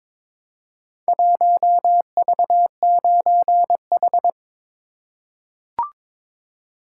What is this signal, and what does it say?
Morse code, heard as a keyed tone, about 22 wpm, 700 Hz. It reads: 1V9H
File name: Four Letter-Number Combinations - 22wpm